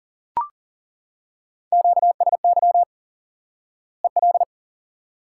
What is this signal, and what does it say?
Morse code, heard as a keyed tone, about 40 wpm, 700 Hz. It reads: QSY EL
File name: IQE Course 3 - Lesson 89 - YAGI - Two Elements - 40wpm